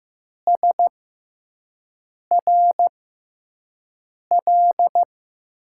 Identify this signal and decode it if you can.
Morse code, heard as a keyed tone, about 15 wpm, spaced as for 5 wpm, 700 Hz. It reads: SRL
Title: USA Call Signs - 5 Characters - 6x Character Spacing - 15wpm